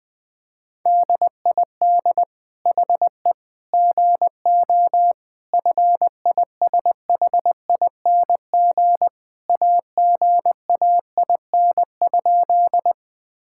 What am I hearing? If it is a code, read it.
Morse code, heard as a keyed tone, about 20 wpm, 700 Hz. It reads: DID HE GO FISHING AGAIN?
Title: Sentences from Top 1000 Words - Review Part 5 - 20wpm